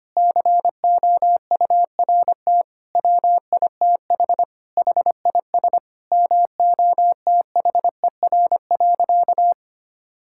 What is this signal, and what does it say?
Morse code, heard as a keyed tone, about 25 wpm, 700 Hz. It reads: COURT WITH HIS MOTHER.
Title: Sentences from Top 1600 Words 25wpm